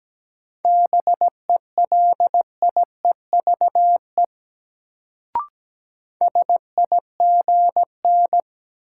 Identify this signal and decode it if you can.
Morse code, heard as a keyed tone, about 17 wpm, 700 Hz. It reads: BELIEVE SIGN
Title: Sentences from Top 1000 Words - New Word Drill - 17wpm